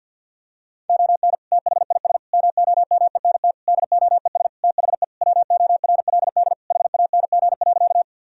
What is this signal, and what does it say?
Morse code, heard as a keyed tone, about 50 wpm, 700 Hz. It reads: ON THIS MOMENT DOES THE WORLD HANG.
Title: Sentences from Top 1600 Words 50wpm